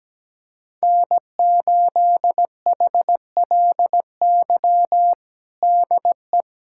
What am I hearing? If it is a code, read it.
Morse code, heard as a keyed tone, about 17 wpm, 700 Hz. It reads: N8HLY DE
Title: USA Callsign DE Callsign - 5 Characters - 17wpm